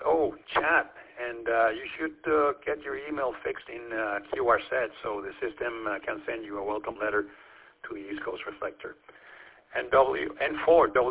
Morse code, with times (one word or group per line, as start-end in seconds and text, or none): none